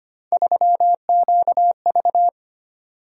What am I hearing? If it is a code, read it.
Morse code, heard as a keyed tone, about 25 wpm, 700 Hz. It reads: 3QV
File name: Three Letter-Number Combinations - 25wpm